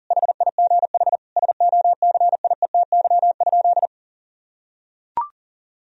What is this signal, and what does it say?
Morse code, heard as a keyed tone, about 40 wpm, 700 Hz. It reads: HIGH SOCIETY?
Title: Sentences from Top 1000 Words - Review Part 4 - 40wpm